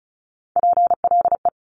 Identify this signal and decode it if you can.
Morse code, heard as a keyed tone, about 36 wpm, 700 Hz. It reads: PLE
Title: Top Three Letter Combinations in Words 35wpm